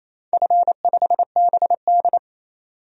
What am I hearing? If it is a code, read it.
Morse code, heard as a keyed tone, about 28 wpm, 700 Hz. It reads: F5BD
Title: DXCC-100 Call Signs - 4 Characters - 28wpm